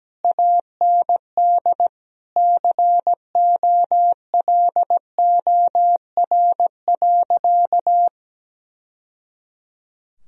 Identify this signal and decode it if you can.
Morse code, heard as a keyed tone, about 17 wpm, 700 Hz. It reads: AND COLOR.
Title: Sentences from Top 1000 Words - Review Part 2 - 17wpm